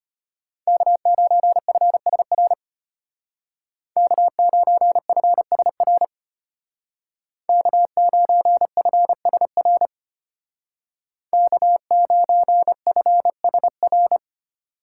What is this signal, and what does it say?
Morse code, heard as a keyed tone, about 33 wpm, 700 Hz. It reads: K9FSR K9FSR K9FSR K9FSR